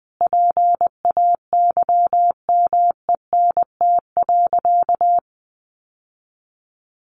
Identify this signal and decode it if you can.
Morse code, heard as a keyed tone, about 20 wpm, 700 Hz. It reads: PAYMENT.
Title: Sentences from Top 1200 Words 20wpm